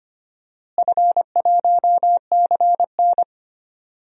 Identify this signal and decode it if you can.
Morse code, heard as a keyed tone, about 25 wpm, 700 Hz. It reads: F1CN
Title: DXCC-100 Call Signs - 4 Characters - Repeated 3x - 25wpm